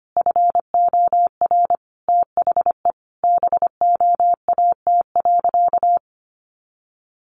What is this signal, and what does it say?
Morse code, heard as a keyed tone, about 25 wpm, 700 Hz. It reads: FOR THE BOAT.
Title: Sentences from Top 1100 Words 25wpm